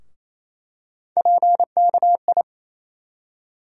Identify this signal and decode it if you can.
Morse code, heard as a keyed tone, about 28 wpm, 700 Hz. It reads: PKI
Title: USA Call Sign Postfixes - 3 Letters 28wpm